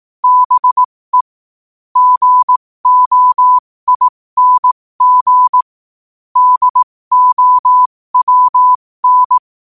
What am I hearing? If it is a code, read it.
Morse code, heard as a keyed tone, about 18 wpm, spaced as for 15 wpm, 1000 Hz. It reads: BE GOING DOWN
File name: qotdfiles / 15wpm